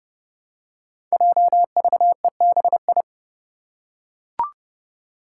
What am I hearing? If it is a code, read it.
Morse code, heard as a keyed tone, about 30 wpm, 700 Hz. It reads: JVEBI
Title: Five Challenging Letter-Number Combinations 30wpm